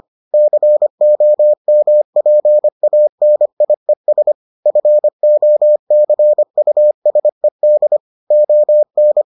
Morse code, heard as a keyed tone, about 25 wpm, 600 Hz. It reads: COMPANIES FOCUSED ON